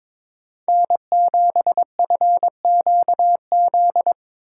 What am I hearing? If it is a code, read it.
Morse code, heard as a keyed tone, about 22 wpm, 700 Hz. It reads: N7FQZ
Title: USA Call Signs - 5 Characters - 2x Character Spacing - 22wpm